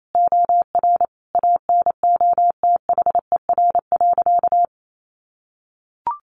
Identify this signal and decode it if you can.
Morse code, heard as a keyed tone, about 28 wpm, 700 Hz. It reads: OR ANOTHER.
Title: Sentences from Top 1400 Words 28wpm